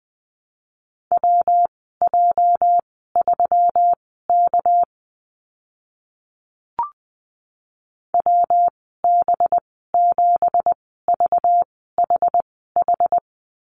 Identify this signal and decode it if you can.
Morse code, heard as a keyed tone, about 20 wpm, spaced as for 14 wpm, 700 Hz. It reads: WJ3K WB7VHH
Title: 207 Lesson 33g - Letter j - Call Signs 20wpm